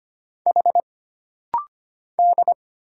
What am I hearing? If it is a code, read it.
Morse code, heard as a keyed tone, about 25 wpm, 700 Hz. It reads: H D
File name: Single Character (Letter:Number:Punctuation:Prosign) - Rapid-Fire - 25wpm